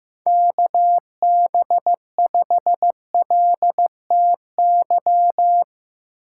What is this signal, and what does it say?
Morse code, heard as a keyed tone, about 15 wpm, 700 Hz. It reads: KB5LTY
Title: USA Call Signs - 4x Character Spacing - 15wpm